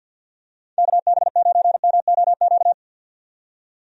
Morse code, heard as a keyed tone, about 51 wpm, 700 Hz. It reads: KB9MOQ
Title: USA Call Signs - 6 Characters - 50wpm